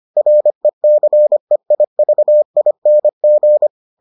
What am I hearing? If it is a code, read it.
Morse code, heard as a keyed tone, about 25 wpm, 600 Hz. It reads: RECEIVING